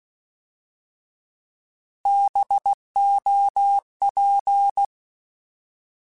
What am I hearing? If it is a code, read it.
Morse code, heard as a keyed tone, about 16 wpm, 800 Hz. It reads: BOP
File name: Alphabetical / Alphabetical mp3s / Bs @15